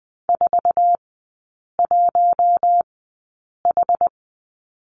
Morse code, heard as a keyed tone, about 20 wpm, spaced as for 8 wpm, 690 Hz. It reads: 41H
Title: 060 Lesson 11e - Letter h - Three Characters 20wpm